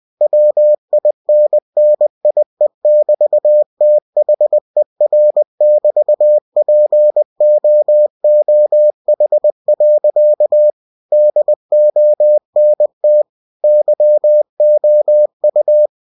Morse code, heard as a keyed tone, about 20 wpm, 600 Hz. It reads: WINNIE<BT>THER<BT>POOH. DONT YOU